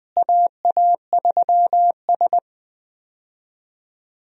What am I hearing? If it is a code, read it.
Morse code, heard as a keyed tone, about 20 wpm, 700 Hz. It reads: AA3S
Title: USA Call Signs - 4 Characters - Repeated 2x - 20wpm